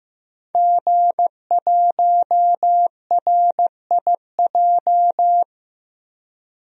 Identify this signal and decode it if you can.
Morse code, heard as a keyed tone, about 15 wpm, 700 Hz. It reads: G1RIJ
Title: DXCC-100 Call Signs - 5 Characters - 15wpm